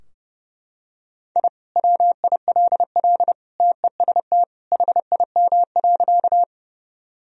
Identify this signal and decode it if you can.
Morse code, heard as a keyed tone, about 30 wpm, 700 Hz. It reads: I WILL TEST HIM.
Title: Sentences from Top 1000 Words - Review Part 3 - 30wpm